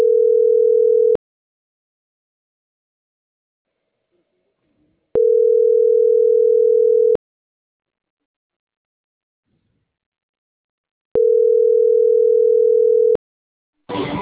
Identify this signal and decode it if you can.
Ringback tone (North America), 440 + 480 Hz.